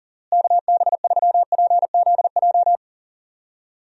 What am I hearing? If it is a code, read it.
Morse code, heard as a keyed tone, about 40 wpm, 700 Hz. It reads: KB3PZJ
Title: USA Call Signs - 6 Characters - Repeated 2x - 40wpm